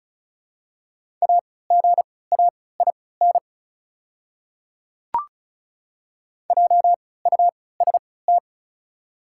Morse code, heard as a keyed tone, about 36 wpm, spaced as for 20 wpm, 700 Hz. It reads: AGAIN JUST
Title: 3x Character Spacing - Top 200 Words - 35wpm